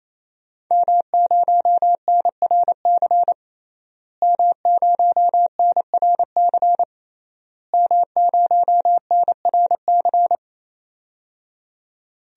Morse code, heard as a keyed tone, about 28 wpm, 700 Hz. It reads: M0NRC M0NRC M0NRC
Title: DXCC-100 Call Signs - 5 Characters - Repeated 3x - 28wpm